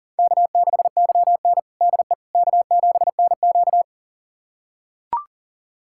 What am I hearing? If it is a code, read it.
Morse code, heard as a keyed tone, about 40 wpm, 700 Hz. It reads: K6YN DE K7NQ
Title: USA Callsign DE Callsign - 4 Characters - 40wpm